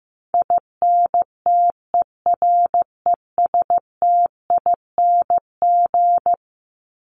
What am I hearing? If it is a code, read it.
Morse code, heard as a keyed tone, about 15 wpm, 700 Hz. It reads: INTERESTING
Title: ING Suffix - Top 100 Words with Base Words 15wpm